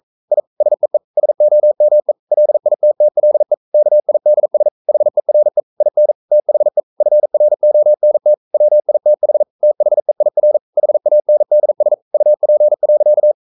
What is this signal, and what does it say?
Morse code, heard as a keyed tone, about 42 wpm, 600 Hz. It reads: I SEE SOME LITTLE KIDS HERE IN THE FRONT WITH THEIR HANDS UP.
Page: Romanen Never af Ken Follet på engelsk.